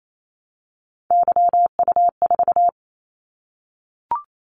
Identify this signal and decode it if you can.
Morse code, heard as a keyed tone, about 28 wpm, 700 Hz. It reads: YU4
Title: Callsign Prefixes for Top 25 DXCC - 28wpm